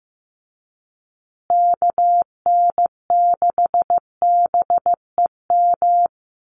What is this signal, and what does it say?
Morse code, heard as a keyed tone, about 15 wpm, 700 Hz. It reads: KN6BEM
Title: USA Call Signs Repeated 3x 15wpm